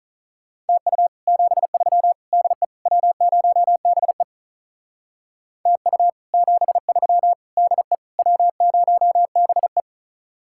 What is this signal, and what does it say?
Morse code, heard as a keyed tone, about 45 wpm, 700 Hz. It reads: TU 73 DE W0BE TU 73 DE W0BE